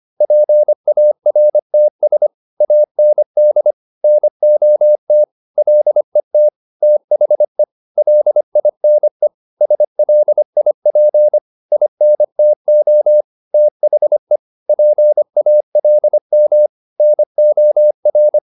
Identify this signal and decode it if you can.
Morse code, heard as a keyed tone, about 25 wpm, 600 Hz. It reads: PARTS AND NOT LET THE LINE SLIP INTO THE PALM NOR